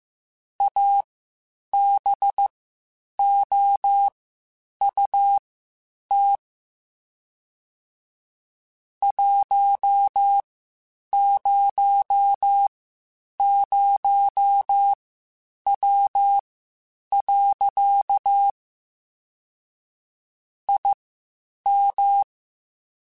Morse code, heard as a keyed tone, about 15 wpm, spaced as for 8 wpm, 800 Hz. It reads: ABOUT 100W. IM
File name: (spaced) / QSOs Spaced-mp3s / WD0FBT 1 @13 sp